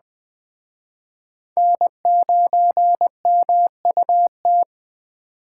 Morse code, heard as a keyed tone, about 21 wpm, 700 Hz. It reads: N9MUT